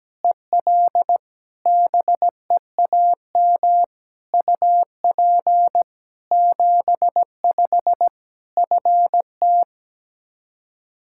Morse code, heard as a keyed tone, about 17 wpm, 700 Hz. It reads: EL BEAM UP 75 FT
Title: K1USN SST QSO Practice 17wpm